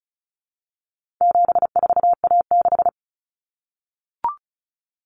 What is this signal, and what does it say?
Morse code, heard as a keyed tone, about 34 wpm, 700 Hz. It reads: 74A6
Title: Common Cut Numbers - 4-Digit 35wpm